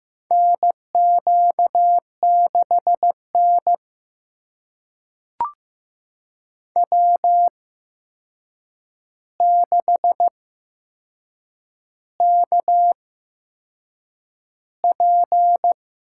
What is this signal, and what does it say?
Morse code, heard as a keyed tone, about 15 wpm, spaced as for 6 wpm, 700 Hz. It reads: NQ6N W6KP
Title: USA Call Signs - 4 Characters - 8x Character Spacing - 15wpm